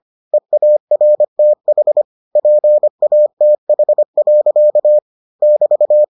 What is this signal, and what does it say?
Morse code, heard as a keyed tone, about 25 wpm, 600 Hz. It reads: EARTH PATH. <BT>